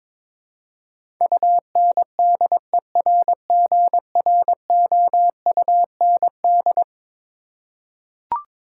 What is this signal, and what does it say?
Morse code, heard as a keyed tone, about 23 wpm, 700 Hz. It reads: UNDERGROUND
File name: practice / 22wpm / Eleven Letter Words - 22wpm